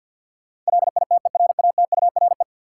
Morse code, heard as a keyed tone, about 46 wpm, 700 Hz. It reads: LITERATURE
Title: Sentences from Top 600 Words - New Word Drill - 50wpm